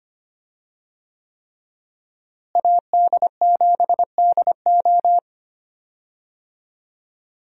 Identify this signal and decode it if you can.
Morse code, heard as a keyed tone, about 25 wpm, 700 Hz. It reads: AD7DO
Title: USA Call Signs - 5 Characters - Repeated 3x - 25wpm